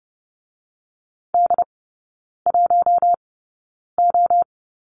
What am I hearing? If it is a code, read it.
Morse code, heard as a keyed tone, about 30 wpm, spaced as for 9 wpm, 700 Hz. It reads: D1O
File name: 081 Lesson 14e - Number 2 - Three Characters 30wpm